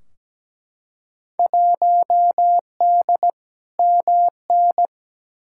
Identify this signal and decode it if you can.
Morse code, heard as a keyed tone, about 17 wpm, 700 Hz. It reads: 1D MN
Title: Field Day Exchanges - 17wpm